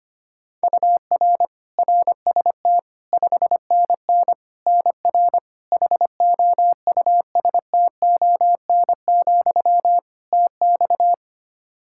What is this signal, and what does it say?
Morse code, heard as a keyed tone, about 25 wpm, 700 Hz. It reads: UR RST 5NN NR HOUSTON, TX
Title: K1USN SST QSO Practice 25wpm